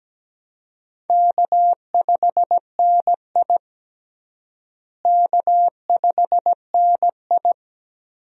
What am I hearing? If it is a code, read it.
Morse code, heard as a keyed tone, about 17 wpm, 700 Hz. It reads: K5NI K5NI
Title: USA Call Signs - 4 Characters - Repeated 3x - 17wpm